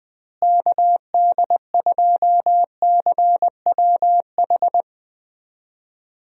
Morse code, heard as a keyed tone, about 20 wpm, 700 Hz. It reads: KD2CWH
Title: USA Call Signs - 6 Characters - Repeated 3x - 20wpm